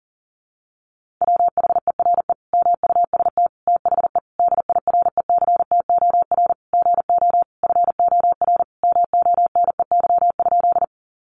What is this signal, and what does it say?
Morse code, heard as a keyed tone, about 40 wpm, 700 Hz. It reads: WHERE MUST THE DIRECTOR GO FOR MONEY?
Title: Sentences from Top 1300 Words 40wpm